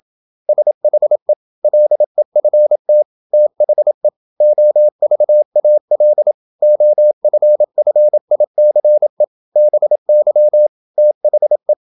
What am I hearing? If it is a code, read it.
Morse code, heard as a keyed tone, about 27 wpm, 600 Hz. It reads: SHE LEFT THE OVAL OFFICE BY THE
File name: Never 0027wpm